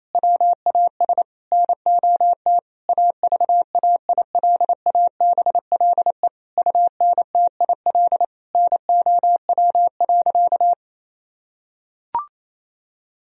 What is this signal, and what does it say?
Morse code, heard as a keyed tone, about 27 wpm, 700 Hz. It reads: WAS NOT AVAILABLE UNTIL NOW.
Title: Sentences from Top 1000 Words - Review Part 3 - 28wpm